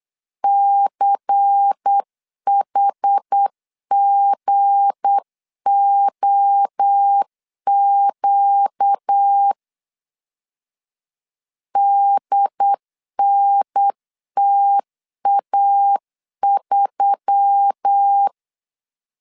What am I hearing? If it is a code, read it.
Morse code, heard as a keyed tone, about 8 wpm, 800 Hz. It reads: CHGOQ DNTA3
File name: LEZIONI / 7 GF9Z / 7 G 8wpm 3118